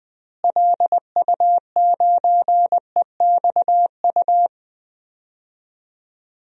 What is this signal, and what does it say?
Morse code, heard as a keyed tone, about 20 wpm, 700 Hz. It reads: LU9EXU
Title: DXCC-100 Call Signs - 6 Characters - Repeated 3x - 20wpm